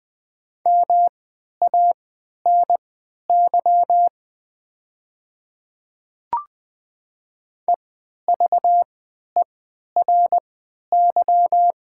Morse code, heard as a keyed tone, about 20 wpm, spaced as for 11 wpm, 700 Hz. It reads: MANY EVERY
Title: 3x Character Spacing - Top 300 Words - 20wpm